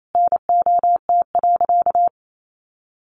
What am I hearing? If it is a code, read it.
Morse code, heard as a keyed tone, about 28 wpm, 700 Hz. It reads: NOT.
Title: Sentences from Top 700 Words 28wpm